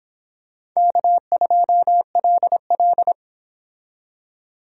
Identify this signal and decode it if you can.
Morse code, heard as a keyed tone, about 25 wpm, 700 Hz. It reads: K2LL